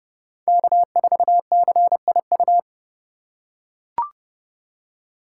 Morse code, heard as a keyed tone, about 30 wpm, 700 Hz. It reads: K4CIU